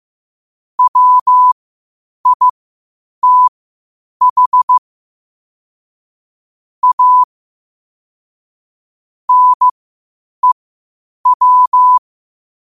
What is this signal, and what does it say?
Morse code, heard as a keyed tone, about 15 wpm, spaced as for 8 wpm, 1000 Hz. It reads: WITH A NEW
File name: qotdfiles / 8wpm